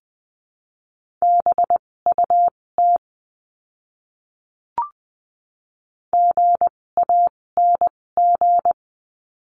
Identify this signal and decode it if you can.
Morse code, heard as a keyed tone, about 20 wpm, spaced as for 16 wpm, 700 Hz. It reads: BUT GANG
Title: 256 Lesson 42a - Words 12-20wpm